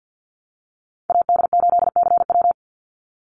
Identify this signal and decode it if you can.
Morse code, heard as a keyed tone, about 50 wpm, 700 Hz. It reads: AD8CW
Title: Call Signs - 50wpm